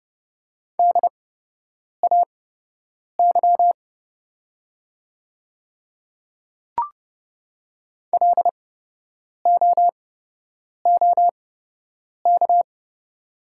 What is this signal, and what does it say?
Morse code, heard as a keyed tone, about 30 wpm, spaced as for 8 wpm, 700 Hz. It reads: DAY LOOK
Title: 8x Character Spacing - Top 300 Words - 30wpm